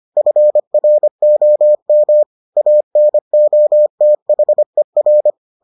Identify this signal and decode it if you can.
Morse code, heard as a keyed tone, about 25 wpm, 600 Hz. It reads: FROM ANOTHER